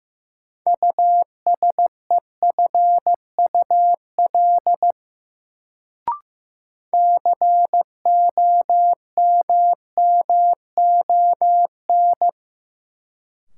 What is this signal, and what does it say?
Morse code, heard as a keyed tone, about 16 wpm, 700 Hz. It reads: USEFUL COMMON
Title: Sentences from Top 700 Words - New Word Drill - 15wpm